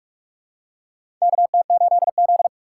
Morse code, heard as a keyed tone, about 47 wpm, 700 Hz. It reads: KT8Z
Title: USA Call Signs - 4 Characters - 6x Character Spacing - 45wpm